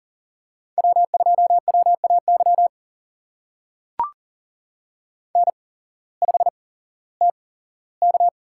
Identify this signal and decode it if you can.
Morse code, heard as a keyed tone, about 42 wpm, spaced as for 19 wpm, 700 Hz. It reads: W2WAY N5TK
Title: USA Call Signs - 8x Character Spacing - 40wpm